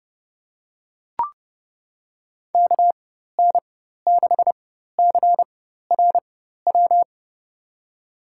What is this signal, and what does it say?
Morse code, heard as a keyed tone, about 30 wpm, spaced as for 14 wpm, 700 Hz. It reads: KN6CRW
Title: USA Call Signs - 4x Character Spacing - 30wpm